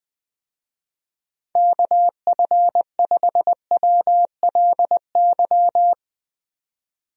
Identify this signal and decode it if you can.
Morse code, heard as a keyed tone, about 21 wpm, 700 Hz. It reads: KF5WLY
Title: USA Call Signs - 6 Characters - 20wpm